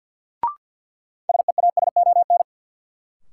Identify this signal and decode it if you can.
Morse code, heard as a keyed tone, about 53 wpm, 700 Hz. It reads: SEASON
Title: Sentences from Top 1400 Words - New Word Drill Alt - 50wpm